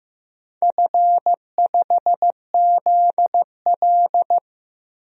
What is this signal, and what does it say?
Morse code, heard as a keyed tone, about 15 wpm, 700 Hz. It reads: F5ZL
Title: DXCC-100 Call Signs - 4 Characters - 15wpm